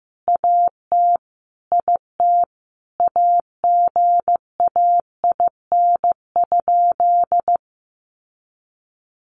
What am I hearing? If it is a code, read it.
Morse code, heard as a keyed tone, about 15 wpm, 700 Hz. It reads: AT IT AGAIN?
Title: Sentences from Top 200 Words 15wpm